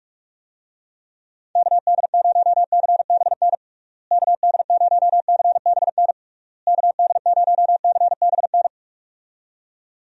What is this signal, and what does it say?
Morse code, heard as a keyed tone, about 47 wpm, 700 Hz. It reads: KD0CBN KD0CBN KD0CBN
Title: USA Call Signs - 6 Characters - Repeated 3x - 45wpm